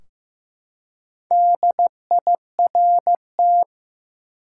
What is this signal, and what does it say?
Morse code, heard as a keyed tone, about 15 wpm, 700 Hz. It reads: DIRT
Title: Four Letter Words - 15wpm